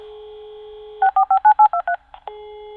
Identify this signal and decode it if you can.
DTMF, heard as telephone touch tones, tones of about 80 ms. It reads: 45 23